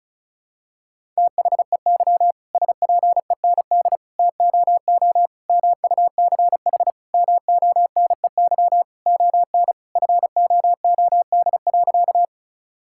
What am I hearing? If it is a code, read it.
Morse code, heard as a keyed tone, about 36 wpm, 700 Hz. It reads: THEY SPEND TOO MUCH MONEY ON FOOD.
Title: Sentences from Top 1000 Words - Review Part 5 - 35wpm